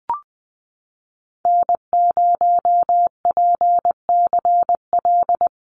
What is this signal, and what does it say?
Morse code, heard as a keyed tone, about 20 wpm, 700 Hz. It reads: N0PCL
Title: NA SOTA Activator Callsigns 20wpm